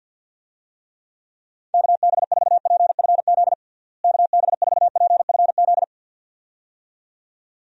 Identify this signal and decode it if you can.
Morse code, heard as a keyed tone, about 46 wpm, 700 Hz. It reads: KB4PFZ KB4PFZ
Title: USA Call Signs - 6 Characters - Repeated 2x - 50wpm